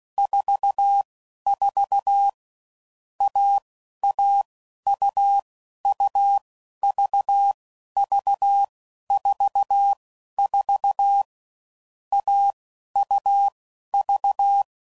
Morse code, heard as a keyed tone, about 16 wpm, 800 Hz. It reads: 4 4 A A U U V V 4 4 A U V
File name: Combinations / Difficult Combos / AUV AUV4